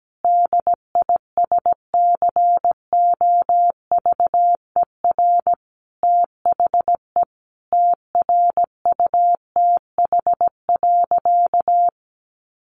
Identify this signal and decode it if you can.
Morse code, heard as a keyed tone, about 17 wpm, 700 Hz. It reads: DISCOVER THE TRUTH.
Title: Sentences from Top 1100 Words 17wpm